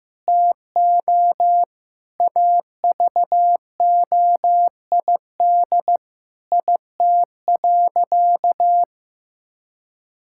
Morse code, heard as a keyed tone, about 16 wpm, 700 Hz. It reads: TO AVOID IT.
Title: Sentences from Top 1000 Words - Review Part 6 - 15wpm